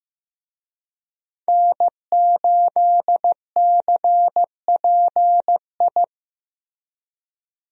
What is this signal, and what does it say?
Morse code, heard as a keyed tone, about 16 wpm, 700 Hz. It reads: N8CPI
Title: USA Call Signs - 5 Characters - Repeated 2x - 15wpm